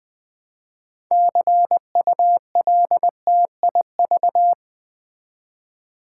Morse code, heard as a keyed tone, about 20 wpm, 700 Hz. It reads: CULTIV
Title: Build A Word - 3 Letters At A Time - 9-Letter Words - 20wpm